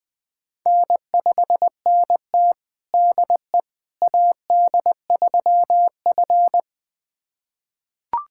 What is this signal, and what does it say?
Morse code, heard as a keyed tone, about 20 wpm, 700 Hz. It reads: N5NT DE AD3F
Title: USA Callsign DE Callsign - 4 Characters - 20wpm